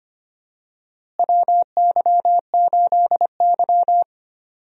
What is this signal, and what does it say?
Morse code, heard as a keyed tone, about 25 wpm, 700 Hz. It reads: WY8Y
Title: USA Call Signs - 4 Characters - Repeated 3x - 25wpm